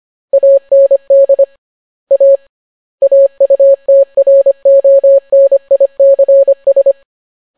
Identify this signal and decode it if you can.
Morse code, heard as a keyed tone, about 25 wpm, 550 Hz. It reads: AND A AUTRONICS